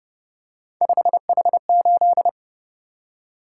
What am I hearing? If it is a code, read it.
Morse code, heard as a keyed tone, about 28 wpm, 700 Hz. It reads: 5H8